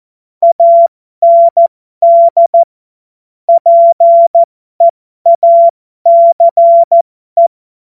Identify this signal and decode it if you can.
Morse code, heard as a keyed tone, about 14 wpm, spaced as for 12 wpm, 680 Hz. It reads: AND PEACE
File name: Morsekurs / Witze